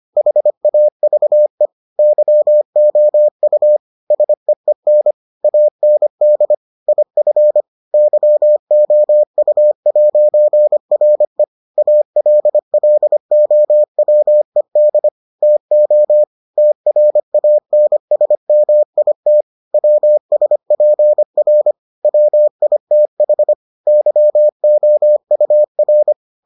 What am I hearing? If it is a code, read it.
Morse code, heard as a keyed tone, about 25 wpm, 600 Hz. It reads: HAVE YOU SEEN AND IF YOU'RE ALLOWED TO TRANSMIT WSPR WITH YOUR